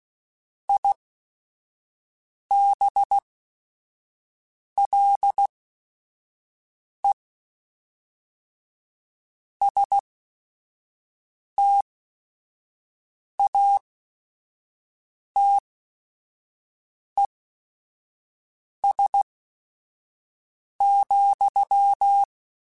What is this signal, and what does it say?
Morse code, heard as a keyed tone, about 16 wpm, spaced as for 5 wpm, 800 Hz. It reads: IBLE STATES,
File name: Technical Subjects / Technical Subjects-mp3s / AND Gate @5